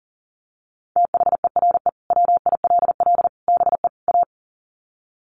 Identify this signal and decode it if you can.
Morse code, heard as a keyed tone, about 40 wpm, 700 Hz. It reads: THERE WILL BE A
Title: Sets of 4 Words 40wpm